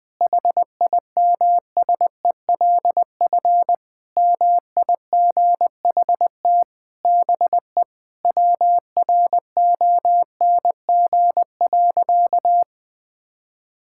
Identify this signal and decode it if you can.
Morse code, heard as a keyed tone, about 21 wpm, 700 Hz. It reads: HIMSELF MIGHT BE WRONG.
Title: Sentences from Top 1600 Words 20wpm